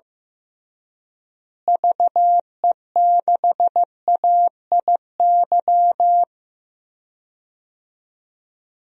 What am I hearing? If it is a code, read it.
Morse code, heard as a keyed tone, about 15 wpm, 700 Hz. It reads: VE6AIY
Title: DXCC-100 Call Signs - 6 Characters - Repeated 3x - 15wpm